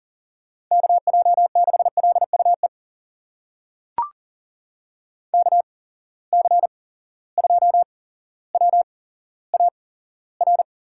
Morse code, heard as a keyed tone, about 40 wpm, spaced as for 17 wpm, 700 Hz. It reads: KJ6LUE KC2WAR
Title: USA Call Signs - 6 Characters - 8x Character Spacing - 40wpm